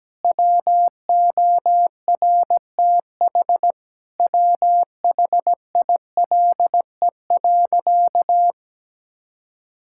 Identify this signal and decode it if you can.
Morse code, heard as a keyed tone, about 17 wpm, 700 Hz. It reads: WORTH WHILE.
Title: Sentences from Top 1000 Words - Review Part 5 - 17wpm